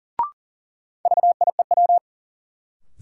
Morse code, heard as a keyed tone, about 40 wpm, 700 Hz. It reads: VIEW
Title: Sets of 5 Words - New Word Drill - 40wpm